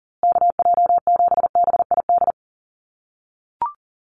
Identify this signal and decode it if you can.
Morse code, heard as a keyed tone, about 40 wpm, 700 Hz. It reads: KJ7BID
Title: USA Call Signs Repeated 3x 40wpm